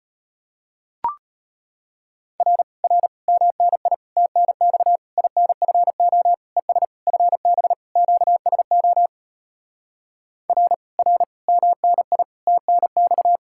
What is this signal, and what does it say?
Morse code, heard as a keyed tone, about 33 wpm, 700 Hz. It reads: R R MNI TNX INFO ES FB QSO R R MNI TNX